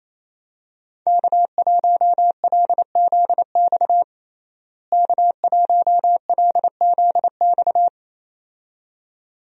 Morse code, heard as a keyed tone, about 27 wpm, 700 Hz. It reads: K1LZX K1LZX
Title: USA Call Signs - 5 Characters - Repeated 2x - 28wpm